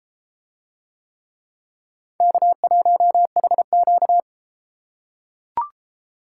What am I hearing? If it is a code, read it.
Morse code, heard as a keyed tone, about 31 wpm, 700 Hz. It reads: K1HQ